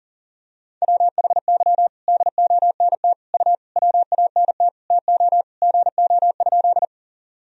Morse code, heard as a keyed tone, about 40 wpm, 700 Hz. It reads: WHY DONT U WANT TO GO?
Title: Sentences from Top 300 Words and Abbreviations 40wpm